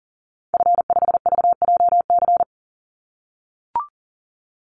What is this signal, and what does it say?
Morse code, heard as a keyed tone, about 40 wpm, 700 Hz. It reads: F5VJC
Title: Call Signs - 40wpm